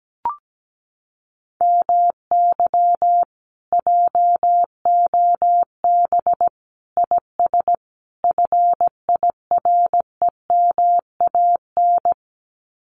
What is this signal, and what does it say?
Morse code, heard as a keyed tone, about 17 wpm, 700 Hz. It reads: MY JOB IS FIREMAN
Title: Common QSO Phrases Encore 17wpm